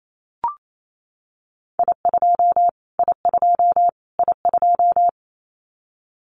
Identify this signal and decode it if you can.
Morse code, heard as a keyed tone, about 28 wpm, 700 Hz. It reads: I2 I2 I2
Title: Callsign Prefixes for Top 25 DXCC - Repeated 3x - 28wpm